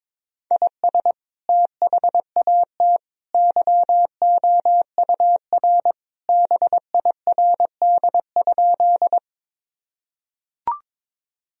Morse code, heard as a keyed tone, about 22 wpm, 700 Hz. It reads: IS THAT YOUR BIRD?
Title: Sentences from Top 1000 Words - Review Part 2 - 22wpm